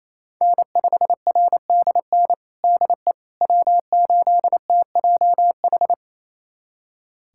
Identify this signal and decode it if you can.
Morse code, heard as a keyed tone, about 28 wpm, 700 Hz. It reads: N5RDN DE W8TJH
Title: USA Callsign DE Callsign - 5 Characters - 28wpm